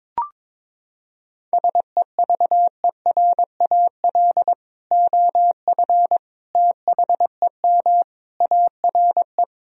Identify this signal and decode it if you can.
Morse code, heard as a keyed tone, about 22 wpm, 700 Hz. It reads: SEVERAL OF THEM ARE
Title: Sentences from Top 1000 Words - Review Part 2 - 22wpm